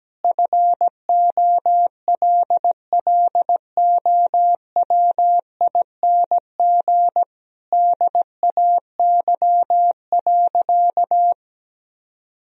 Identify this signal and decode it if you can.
Morse code, heard as a keyed tone, about 17 wpm, 700 Hz. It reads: FOLLOWING DAY.
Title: Sentences from Top 1000 Words - Review Part 4 - 17wpm